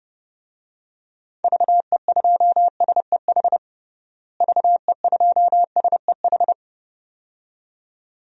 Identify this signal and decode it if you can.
Morse code, heard as a keyed tone, about 32 wpm, 700 Hz. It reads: VE2SEH VE2SEH
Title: DXCC-100 Call Signs - 6 Characters - Repeated 2x - 30wpm